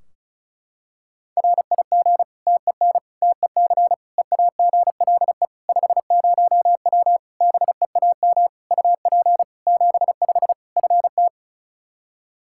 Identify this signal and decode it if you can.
Morse code, heard as a keyed tone, about 36 wpm, 700 Hz. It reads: RIG TEN TEC EAGLE 50W BEAM UP 75 FT
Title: Practice QSO - Exchange 2 - Info - 35wpm